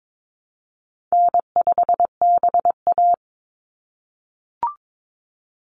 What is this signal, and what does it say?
Morse code, heard as a keyed tone, about 22 wpm, 700 Hz. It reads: N5BA
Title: Easy Call Signs - 22wpm